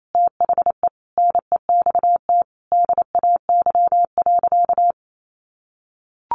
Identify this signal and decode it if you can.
Morse code, heard as a keyed tone, about 28 wpm, 700 Hz. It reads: THE NEXT DAY.
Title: Sentences from Top 1400 Words 28wpm